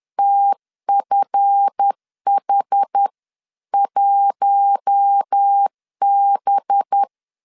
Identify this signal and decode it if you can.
Morse code, heard as a keyed tone, about 10 wpm, 800 Hz. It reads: TFH1B